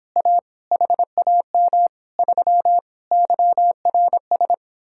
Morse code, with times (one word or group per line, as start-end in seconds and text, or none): 0.2-0.4 A
0.7-1.9 HAM
2.2-2.8 3
3.1-4.5 YRS